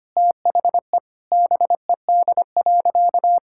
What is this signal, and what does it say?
Morse code, heard as a keyed tone, about 25 wpm, 700 Hz. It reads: THE BED.
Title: Sentences from Top 1600 Words 25wpm